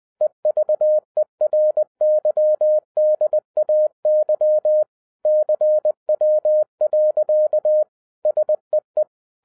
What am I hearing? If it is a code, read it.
Morse code, heard as a keyed tone, about 20 wpm, 610 Hz. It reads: EVERYDAY CW. SEE
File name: 2011-07-mp3 / n43 Czech Army Morse Keys 20wpm